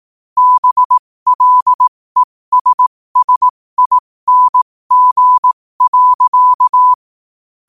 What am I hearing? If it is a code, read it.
Morse code, heard as a keyed tone, about 18 wpm, spaced as for 15 wpm, 1000 Hz. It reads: BLESSING.
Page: New quotes every day in morse code at 15 Words per minute.